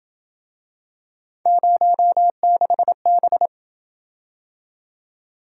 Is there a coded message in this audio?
Yes: Morse code from 1.5 to 3.5 s, heard as a keyed tone, about 27 wpm, 700 Hz. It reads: 06B